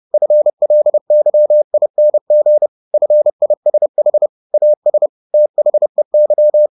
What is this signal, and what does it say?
Morse code, heard as a keyed tone, about 30 wpm, 600 Hz. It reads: FLYING FISH AS THEY
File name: the old man and the sea / the old man and the sea 1 30wpm